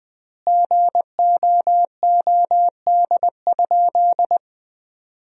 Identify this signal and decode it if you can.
Morse code, heard as a keyed tone, about 20 wpm, 700 Hz. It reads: GOOD?
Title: 3x Word-Spacing - Sentences from Top 200 Words - 20wpm